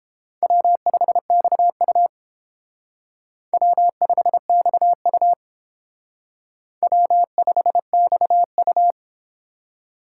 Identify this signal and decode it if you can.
Morse code, heard as a keyed tone, about 28 wpm, 700 Hz. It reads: W5XU W5XU W5XU